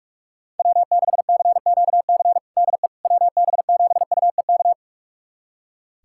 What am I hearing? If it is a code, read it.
Morse code, heard as a keyed tone, about 46 wpm, 700 Hz. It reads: W6CQC DE WB7UEK
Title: USA Callsign DE Callsign 45wpm